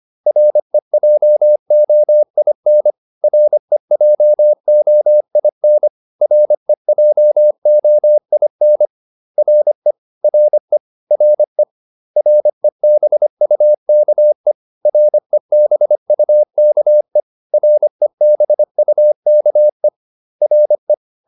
Morse code, heard as a keyed tone, about 25 wpm, 600 Hz. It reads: REJOIN REJOIN REJOIN RE RE RE REBUKE REBUKE REBUKE RE